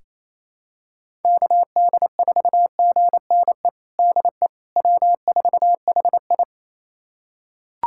Morse code, heard as a keyed tone, about 28 wpm, 700 Hz. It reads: KD4GNE DE W4HI
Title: USA Callsign DE Callsign 28wpm